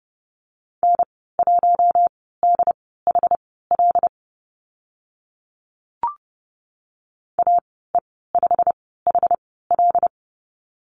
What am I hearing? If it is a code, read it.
Morse code, heard as a keyed tone, about 30 wpm, spaced as for 17 wpm, 700 Hz. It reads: N1DHL AE5HL
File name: Lesson 15g - Number 5 - Call Signs 30wpm